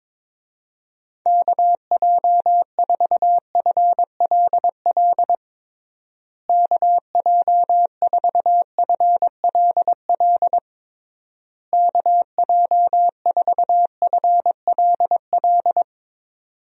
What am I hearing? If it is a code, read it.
Morse code, heard as a keyed tone, about 21 wpm, 700 Hz. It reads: KJ4FLL KJ4FLL KJ4FLL
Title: USA Call Signs - 6 Characters - Repeated 3x - 22wpm